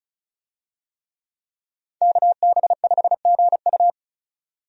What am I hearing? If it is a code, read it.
Morse code, heard as a keyed tone, about 34 wpm, 700 Hz. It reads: KB5GU
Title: USA Call Signs - 5 Characters - 35wpm